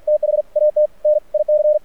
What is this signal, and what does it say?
Morse code, heard as a keyed tone, about 48 wpm, spaced as for 41 wpm, 600 Hz, with radio noise. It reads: THAT NIC
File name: archived / CRSnet